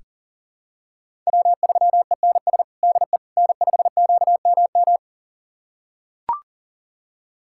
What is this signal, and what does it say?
Morse code, heard as a keyed tone, about 40 wpm, 700 Hz. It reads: W3ENS DE N5QMM
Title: USA Callsign DE Callsign - 5 Characters - 40wpm